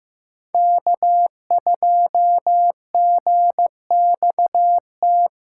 Morse code, heard as a keyed tone, about 15 wpm, 700 Hz. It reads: K2GXT
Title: USA Call Signs - 4x Character Spacing - 15wpm